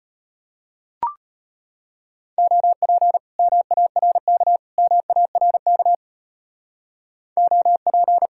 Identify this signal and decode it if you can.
Morse code, heard as a keyed tone, about 33 wpm, 700 Hz. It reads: OP MARK MARK OP